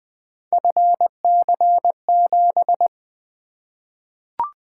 Morse code, heard as a keyed tone, about 20 wpm, 700 Hz. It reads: FC7
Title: Three Letter-Number Combinations - 20wpm